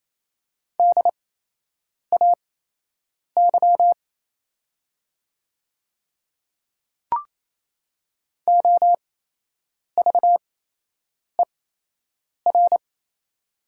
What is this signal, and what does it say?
Morse code, heard as a keyed tone, about 28 wpm, spaced as for 8 wpm, 700 Hz. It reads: DAY OVER
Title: 8x Character Spacing - Top 100 Words - 28wpm